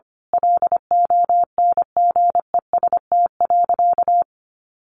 Morse code, heard as a keyed tone, about 25 wpm, 700 Hz. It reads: LONGEST.
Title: A Sound Of Thunder 25wpm